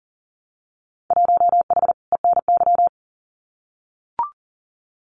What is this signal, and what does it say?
Morse code, heard as a keyed tone, about 38 wpm, 700 Hz. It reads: 1H ENY
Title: Winter Field Day Exchanges 40wpm